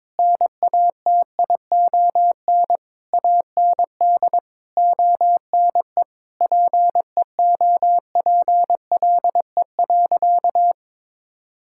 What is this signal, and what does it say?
Morse code, heard as a keyed tone, about 22 wpm, 700 Hz. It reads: NATION AND ONE PEOPLE.
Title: Sentences from Top 1000 Words - Review Part 6 - 22wpm